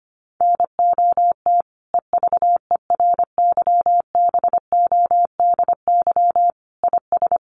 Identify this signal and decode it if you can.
Morse code, heard as a keyed tone, about 25 wpm, 700 Hz. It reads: NOT EVERYBODY IS
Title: Sentences from Top 1400 Words 25wpm